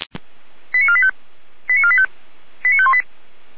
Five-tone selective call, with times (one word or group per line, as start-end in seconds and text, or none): 0.7-1.1 89485
1.7-2.1 89485
2.6-3.0 89518